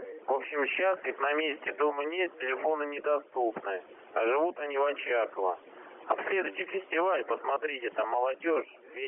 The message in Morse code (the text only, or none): none